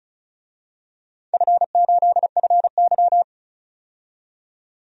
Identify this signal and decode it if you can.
Morse code, heard as a keyed tone, about 36 wpm, 690 Hz. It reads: F8FY
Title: DXCC-100 Call Signs - 4 Characters - 35wpm